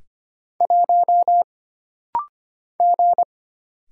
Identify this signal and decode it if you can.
Morse code, heard as a keyed tone, about 25 wpm, 700 Hz. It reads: 1 G
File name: Single Character (Letter:Number:Punctuation:Prosign) - Rapid-Fire - 25wpm